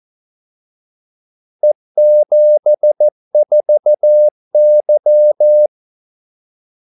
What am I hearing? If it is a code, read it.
Morse code, heard as a keyed tone, about 14 wpm, 600 Hz. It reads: E74Y
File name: Callsigns-14wpm